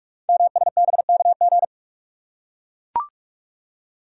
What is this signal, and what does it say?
Morse code, heard as a keyed tone, about 44 wpm, 690 Hz. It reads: MSBKG
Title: Code Groups - 5 Letters - 45wpm